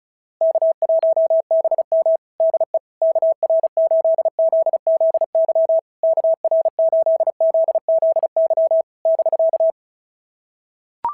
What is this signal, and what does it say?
Morse code, heard as a keyed tone, about 34 wpm, 660 Hz. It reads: K1BM DE KR8ZZY KR8ZZY <BK>
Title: SOTA QSO Practice 35wpm